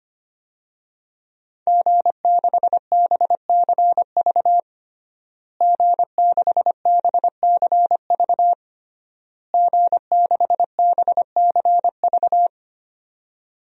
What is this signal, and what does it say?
Morse code, heard as a keyed tone, about 25 wpm, 700 Hz. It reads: G6BCV G6BCV G6BCV
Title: DXCC-100 Call Signs - 5 Characters - Repeated 3x - 25wpm